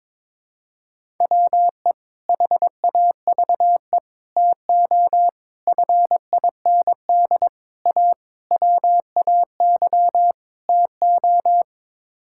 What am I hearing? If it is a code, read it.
Morse code, heard as a keyed tone, about 22 wpm, 700 Hz. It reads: WE HAVE TO FIND A WAY TO
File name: practice / 22wpm / Sets of 7 Words - 22wpm